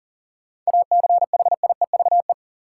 Morse code, heard as a keyed tone, about 37 wpm, 700 Hz. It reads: ACHIEVE
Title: Sentences from Top 1000 Words - Review Part 5 - New Word Drill - 40wpm